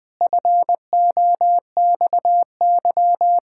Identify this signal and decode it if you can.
Morse code, heard as a keyed tone, about 20 wpm, 700 Hz. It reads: FOXY
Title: Pangram Challenge v2 - 20wpm